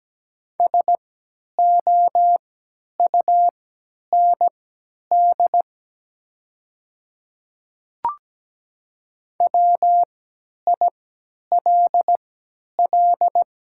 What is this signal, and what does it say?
Morse code, heard as a keyed tone, about 17 wpm, spaced as for 10 wpm, 700 Hz. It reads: SOUND WILL
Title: 3x Character Spacing - Top 100 Words - 17wpm